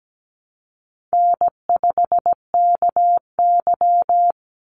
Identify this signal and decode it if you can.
Morse code, heard as a keyed tone, about 17 wpm, 700 Hz. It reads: N5KY
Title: Call Signs - 17wpm